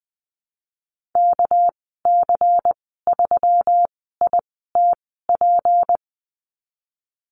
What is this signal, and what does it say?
Morse code, heard as a keyed tone, about 19 wpm, spaced as for 14 wpm, 700 Hz. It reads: KC3ITP
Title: 207 Lesson 33g - Letter j - Call Signs 20wpm